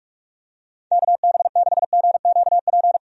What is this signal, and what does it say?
Morse code, heard as a keyed tone, about 46 wpm, 700 Hz. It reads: KB6GQP
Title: USA Call Signs - 6 Characters - 45wpm